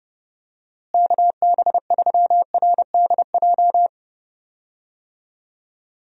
Morse code, heard as a keyed tone, about 32 wpm, 700 Hz. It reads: KB3RDJ
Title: USA Call Signs - 6 Characters - Repeated 2x - 30wpm